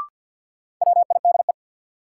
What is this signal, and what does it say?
Morse code, heard as a keyed tone, about 51 wpm, 700 Hz. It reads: WIDE
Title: Sets of 4 Words - New Word Drill - 50wpm